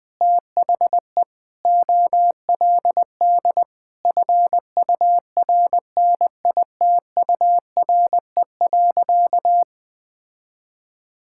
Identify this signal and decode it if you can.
Morse code, heard as a keyed tone, about 20 wpm, 700 Hz. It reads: THE OLD FURNITURE.
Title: Sentences with 4-Word Phrases - 20wpm